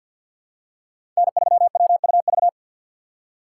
Morse code, heard as a keyed tone, about 52 wpm, 700 Hz. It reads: N3WUV
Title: USA Call Signs - 5 Characters - 3x Character Spacing - 50wpm